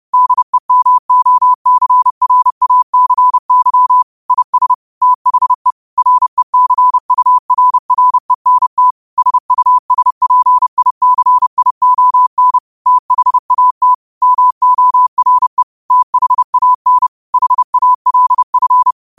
Morse code, heard as a keyed tone, about 30 wpm, 1000 Hz. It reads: DEMOCRACY IS THE RECURRENT SUSPICION THAT MORE THAN HALF
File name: 30wpm